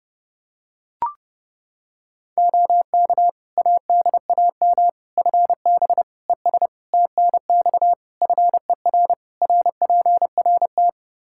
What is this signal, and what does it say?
Morse code, heard as a keyed tone, about 30 wpm, 700 Hz. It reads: OK ADAM FB ES TNX FER RPRT
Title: Practice QSO - Exchange 2 - Info - 30wpm